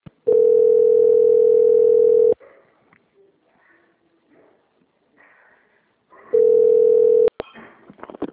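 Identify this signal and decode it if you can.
Ringback tone (North America), 440 + 480 Hz.
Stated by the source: Call recorded on the Jacarepagua - RJ phone system.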